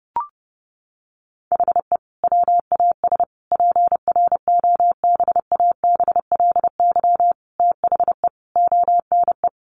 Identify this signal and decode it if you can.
Morse code, heard as a keyed tone, about 28 wpm, 700 Hz. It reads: HE WAS PROBABLY THE ONE
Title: Sentences from Top 1300 Words 30wpm